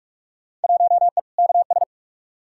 Morse code, heard as a keyed tone, about 43 wpm, 700 Hz. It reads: 1E KS
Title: Field Day Exchanges - 45wpm